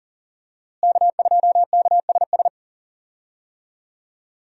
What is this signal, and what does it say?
Morse code, heard as a keyed tone, about 37 wpm, 700 Hz. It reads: K2KSS
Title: USA Call Signs - 5 Characters - Repeated 2x - 40wpm